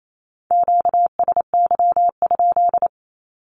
Morse code, heard as a keyed tone, about 28 wpm, 700 Hz. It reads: QSY?
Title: Q-Codes Repeated 3x 28wpm